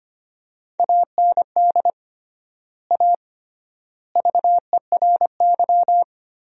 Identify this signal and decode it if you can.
Morse code, heard as a keyed tone, about 25 wpm, 700 Hz. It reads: AND A VERY
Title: 3x Word-Spacing - Sentences from Top 200 Words - 25wpm